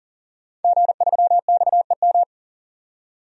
Morse code, heard as a keyed tone, about 40 wpm, 700 Hz. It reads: G3XEM
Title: DXCC-100 Call Signs - 5 Characters - 40wpm